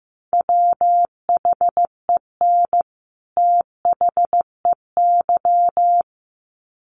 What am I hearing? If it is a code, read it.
Morse code, heard as a keyed tone, about 15 wpm, 700 Hz. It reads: WHEN THEY
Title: Sets of 2 Words 15wpm